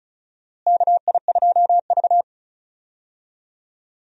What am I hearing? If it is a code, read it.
Morse code, heard as a keyed tone, about 36 wpm, 700 Hz. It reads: KI2V
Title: USA Call Signs - 4 Characters - Repeated 3x - 35wpm